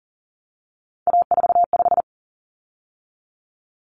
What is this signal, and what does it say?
Morse code, heard as a keyed tone, about 40 wpm, 700 Hz. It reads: A45
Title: Common Cut Numbers - 3-Digit 40wpm